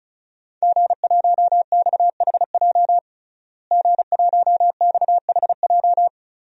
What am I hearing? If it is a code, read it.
Morse code, heard as a keyed tone, about 36 wpm, 690 Hz. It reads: G1XHJ G1XHJ
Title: DXCC-100 Call Signs - 5 Characters - Repeated 3x - 35wpm